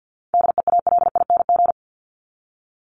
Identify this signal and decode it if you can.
Morse code, heard as a keyed tone, about 49 wpm, 700 Hz. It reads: DEALING
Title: ING Suffix - Top 100 Words with Base Words 50wpm